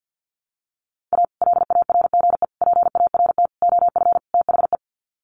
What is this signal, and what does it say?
Morse code, heard as a keyed tone, about 50 wpm, 700 Hz. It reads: A LARGE PART OF THE
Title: Sets of 5 Words Encore2 50wpm